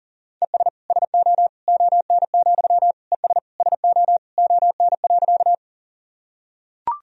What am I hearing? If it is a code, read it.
Morse code, heard as a keyed tone, about 40 wpm, 700 Hz. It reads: ES SO ON, ES SO ON.
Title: Sentences from Top 100 Words and Abbreviations 40wpm